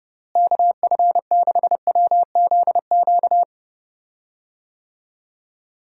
Morse code, heard as a keyed tone, about 30 wpm, 700 Hz. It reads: KF6WZQ
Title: USA Call Signs - 6 Characters - Repeated 3x - 30wpm